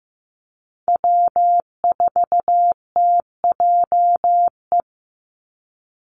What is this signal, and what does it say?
Morse code, heard as a keyed tone, about 15 wpm, 700 Hz. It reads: W4TJE
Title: Call Signs Repeated 2x - 15wpm